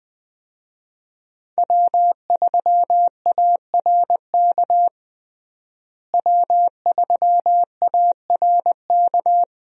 Morse code, heard as a keyed tone, about 20 wpm, 700 Hz. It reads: W3ARK W3ARK
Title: USA Call Signs - 5 Characters - Repeated 2x - 20wpm